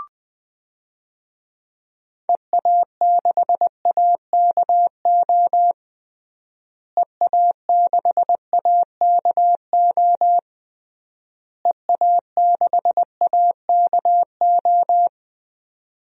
Morse code, heard as a keyed tone, about 20 wpm, 700 Hz. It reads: EA6AKO EA6AKO EA6AKO
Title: DXCC-100 Call Signs - 6 Characters - Repeated 3x - 20wpm